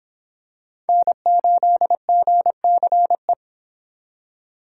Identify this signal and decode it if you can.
Morse code, heard as a keyed tone, about 25 wpm, 700 Hz. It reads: N8GCE